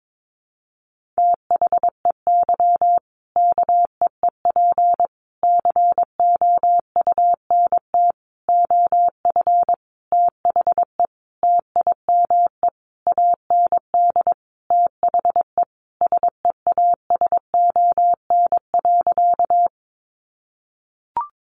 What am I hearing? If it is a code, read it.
Morse code, heard as a keyed tone, about 22 wpm, 700 Hz. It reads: THEY KEEP COUNT OF THE TIME AND THE SEASON.
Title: Sentences from Top 1500 Words 22wpm